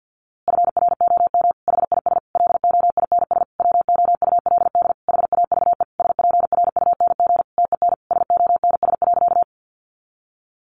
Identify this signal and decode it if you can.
Morse code, heard as a keyed tone, about 48 wpm, 700 Hz. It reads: FROM HIS LOINS WOULD HAVE SPRUNG TEN SONS.
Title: A Sound Of Thunder 50wpm